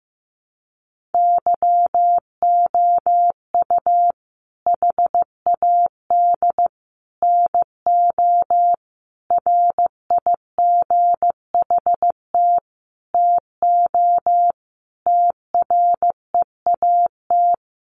Morse code, heard as a keyed tone, about 15 wpm, 700 Hz. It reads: YOU HAD NO RIGHT TO TREAT
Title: Sentences from Top 1300 Words 15wpm